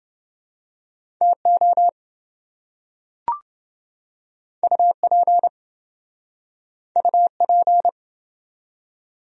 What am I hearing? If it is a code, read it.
Morse code, heard as a keyed tone, about 26 wpm, 700 Hz. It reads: TO UP UP